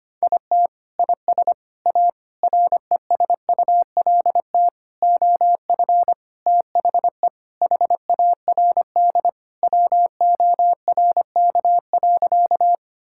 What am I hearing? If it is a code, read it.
Morse code, heard as a keyed tone, about 25 wpm, 700 Hz. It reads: IT IS A RESULT OF THE HARD WORK.
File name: practice / 25wpm / Sentences with 4-Word Phrases - 25wpm